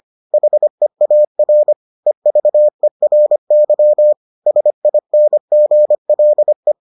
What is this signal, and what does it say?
Morse code, heard as a keyed tone, about 25 wpm, 600 Hz. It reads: HEAR EVERY SINGLE